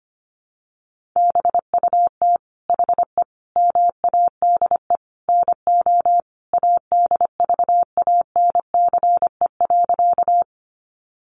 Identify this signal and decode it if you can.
Morse code, heard as a keyed tone, about 25 wpm, 700 Hz. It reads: BUT HE MADE NO ADVANCE.
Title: Sentences from Top 1400 Words 25wpm